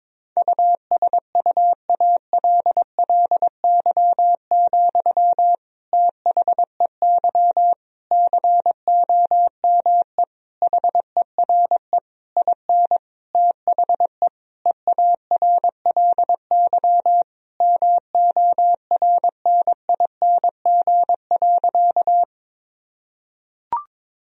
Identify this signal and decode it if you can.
Morse code, heard as a keyed tone, about 22 wpm, 700 Hz. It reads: USUALLY, THEY COME HERE IN THE EARLY MORNING.
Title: Sentences from Top 1000 Words - Review Part 3 - 22wpm